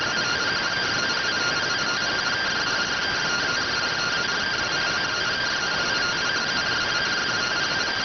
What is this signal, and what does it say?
Two-tone teleprinter signal, about 170 Hz shift.